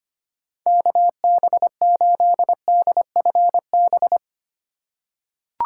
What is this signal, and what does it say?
Morse code, heard as a keyed tone, about 25 wpm, 700 Hz. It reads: KB8DFB
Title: USA Call Signs - 6 Characters - 3x Character Spacing - 25wpm